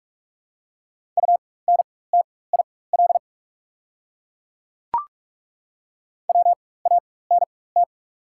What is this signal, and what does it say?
Morse code, heard as a keyed tone, about 48 wpm, spaced as for 21 wpm, 690 Hz. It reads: UNTIL WANT
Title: 4x Character Spacing - Top 300 Words - 45wpm